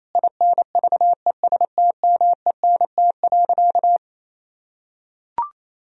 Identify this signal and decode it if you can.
Morse code, heard as a keyed tone, about 28 wpm, 700 Hz. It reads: INVESTMENT.
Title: Sentences from Top 1000 Words - Review Part 4 - 28wpm